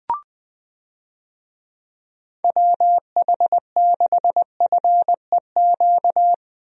Morse code, heard as a keyed tone, about 21 wpm, 700 Hz. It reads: WH6FEQ
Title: USA Call Signs - 6 Characters - Repeated 2x - 20wpm